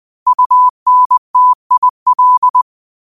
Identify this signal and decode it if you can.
Morse code, heard as a keyed tone, about 20 wpm, 1000 Hz. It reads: UNTIL
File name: qotdfiles / 20wpm